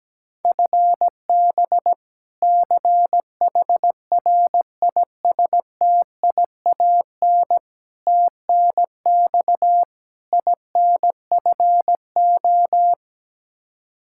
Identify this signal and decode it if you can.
Morse code, heard as a keyed tone, about 17 wpm, 700 Hz. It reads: FB CHRISTIAN TNX INFO
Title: K1USN SST QSO Practice 17wpm